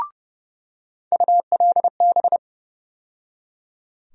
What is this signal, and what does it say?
Morse code, heard as a keyed tone, about 30 wpm, 700 Hz. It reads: ULB
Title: USA Call Sign Postfixes - 3 Letters 30wpm